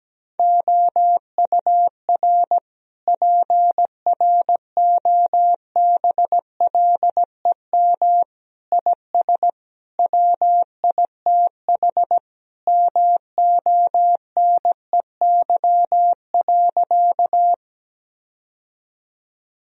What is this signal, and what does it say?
Morse code, heard as a keyed tone, about 17 wpm, 700 Hz. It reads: OUR PROBLEM IS WITH MONEY.
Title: Sentences from Top 1000 Words - Review Part 2 - 17wpm